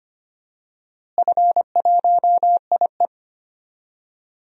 Morse code, heard as a keyed tone, about 25 wpm, 700 Hz. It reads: F1IE
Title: DXCC-100 Call Signs - 4 Characters - Repeated 3x - 25wpm